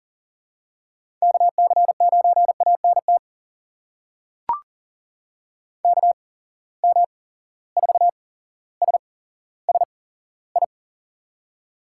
Morse code, heard as a keyed tone, about 39 wpm, spaced as for 17 wpm, 700 Hz. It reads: KC9ANT KM4SSI
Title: USA Call Signs - 6 Characters - 8x Character Spacing - 40wpm